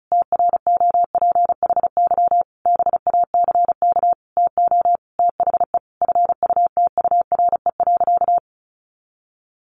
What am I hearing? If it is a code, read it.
Morse code, heard as a keyed tone, about 34 wpm, 700 Hz. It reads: TROPHY BACK TO THE FUTURE.
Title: A Sound Of Thunder 35wpm